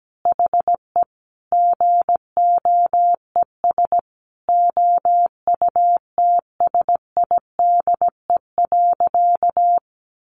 Morse code, heard as a keyed tone, about 17 wpm, 700 Hz. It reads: HE GOES OUTSIDE.
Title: Additional Easy Phrases 17wpm